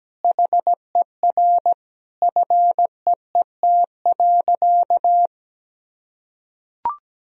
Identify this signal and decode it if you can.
Morse code, heard as a keyed tone, about 17 wpm, 700 Hz. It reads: HER FEET.
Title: Sentences from Top 1000 Words - Review Part 1 - 17wpm